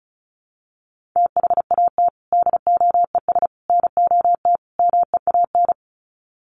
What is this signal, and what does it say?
Morse code, heard as a keyed tone, about 34 wpm, 700 Hz. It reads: THAT DOES NOT MEAN
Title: Sets of 4 Words 35wpm